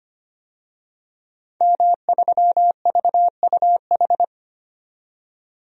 Morse code, heard as a keyed tone, about 25 wpm, 700 Hz. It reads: M3VUH
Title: DXCC-100 Call Signs - 5 Characters - 25wpm